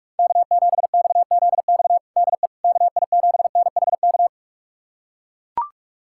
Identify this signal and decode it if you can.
Morse code, heard as a keyed tone, about 42 wpm, 700 Hz. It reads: K7XZX DE KI7NHK
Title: USA Callsign DE Callsign 45wpm